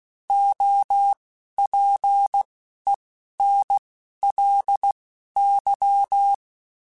Morse code, heard as a keyed tone, about 16 wpm, spaced as for 12 wpm, 800 Hz. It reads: OPENLY